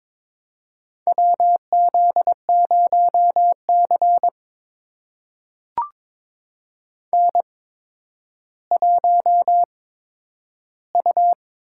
Morse code, heard as a keyed tone, about 22 wpm, spaced as for 11 wpm, 700 Hz. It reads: WZ0C N1U
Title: USA Call Signs - 8x Character Spacing - 22wpm